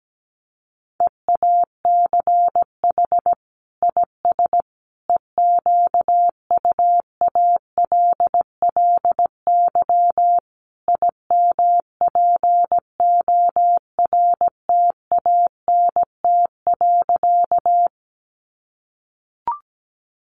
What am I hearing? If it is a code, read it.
Morse code, heard as a keyed tone, about 17 wpm, 700 Hz. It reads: EACH IS EQUALLY IMPORTANT.
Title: Sentences from Top 1400 Words 17wpm